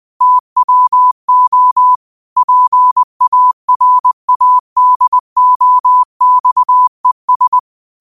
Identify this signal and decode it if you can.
Morse code, heard as a keyed tone, about 20 wpm, 1000 Hz. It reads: TWO PARADOXES
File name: qotdfiles / 20wpm